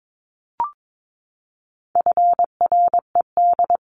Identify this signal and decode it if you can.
Morse code, heard as a keyed tone, about 22 wpm, 700 Hz. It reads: FRED
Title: CWT Exchange 22wpm